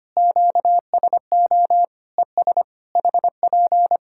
Morse code, heard as a keyed tone, about 25 wpm, 700 Hz. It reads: QSO ES HP
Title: Practice QSO - Exchange 4 - Ending - 25wpm